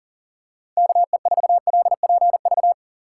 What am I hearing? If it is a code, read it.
Morse code, heard as a keyed tone, about 40 wpm, 700 Hz. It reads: KE4LPV
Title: USA Call Signs - 6 Characters - 3x Character Spacing - 40wpm